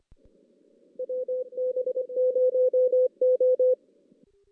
Morse code, heard as a keyed tone, about 25 wpm, 440 Hz. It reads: WB0O
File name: 2016 JAN CW NAQP / 2016 Jan naqp k5ot - 0-5